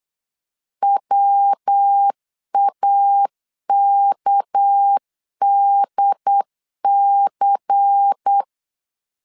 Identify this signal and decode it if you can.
Morse code, heard as a keyed tone, about 8 wpm, 800 Hz. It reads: WAKDC
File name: LEZIONI / 6 PKX0 / 6 PK 8wpm 3003